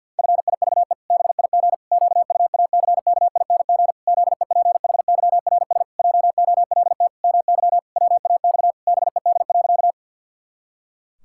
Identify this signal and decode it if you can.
Morse code, heard as a keyed tone, about 50 wpm, 700 Hz. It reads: FIVE BIG QUACKING ZEPHYRS JOLT MY WAX BED.
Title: Pangram Challenge v2 - 50wpm